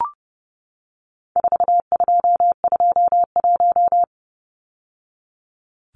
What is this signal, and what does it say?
Morse code, heard as a keyed tone, about 30 wpm, 700 Hz. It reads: 4221
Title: Numbers - 4-Digit 30wpm